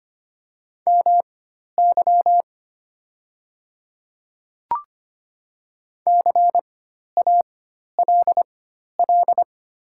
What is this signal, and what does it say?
Morse code, heard as a keyed tone, about 25 wpm, spaced as for 12 wpm, 700 Hz. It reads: MY CALL
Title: 4x Character Spacing - Top 100 Words - 25wpm